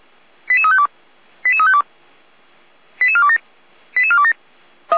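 Five-tone selective call (ZVEI), heard as quick runs of five tones, tones of about 80 ms. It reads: 88352 88352 88427 88427
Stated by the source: ZVEI five-tone selective call received via SDR, recorded on the Feuerwehr Orlamünde fire brigade audio server.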